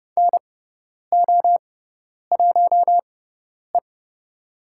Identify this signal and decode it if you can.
Morse code, heard as a keyed tone, about 30 wpm, spaced as for 10 wpm, 700 Hz. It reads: NO1E
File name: Lesson 17g - Letter u - Call Signs 30wpm